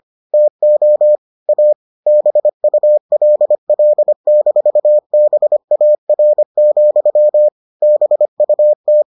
Morse code, heard as a keyed tone, about 25 wpm, 600 Hz. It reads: TO A BULL-BAR, BUT